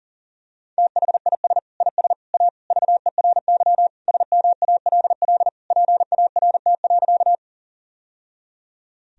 Morse code, heard as a keyed tone, about 40 wpm, 700 Hz. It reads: THIS IS A VERY SMALL PART.
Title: Sentences from Top 1000 Words - Review Part 1 - 40wpm